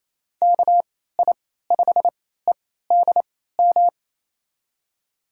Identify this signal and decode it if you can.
Morse code, heard as a keyed tone, about 28 wpm, spaced as for 16 wpm, 700 Hz. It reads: KI5EDM
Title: USA Call Signs - 6 Characters - 3x Character Spacing - 28wpm